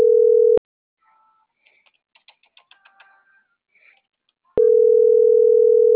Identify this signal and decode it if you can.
Ringback tone (North America), 440 + 480 Hz.
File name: CENTRAL MABE / out noemi.gutierrez